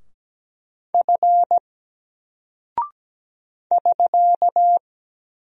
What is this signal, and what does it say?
Morse code, heard as a keyed tone, about 17 wpm, 700 Hz. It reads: F <SK>
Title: Single Character (Letter:Number:Punctuation:Prosign) 17wpm